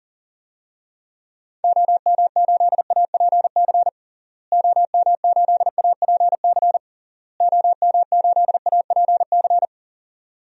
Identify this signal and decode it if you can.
Morse code, heard as a keyed tone, about 40 wpm, 700 Hz. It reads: OM8APC OM8APC OM8APC
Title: DXCC-100 Call Signs - 6 Characters - Repeated 3x - 40wpm